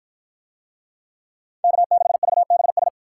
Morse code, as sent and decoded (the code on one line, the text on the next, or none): -.- -.... ...- -... ...
K6VBS